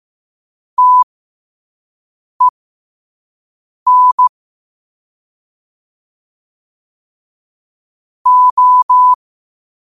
Morse code, heard as a keyed tone, about 15 wpm, spaced as for 5 wpm, 1000 Hz. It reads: TEN O
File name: qotdfiles / 5wpm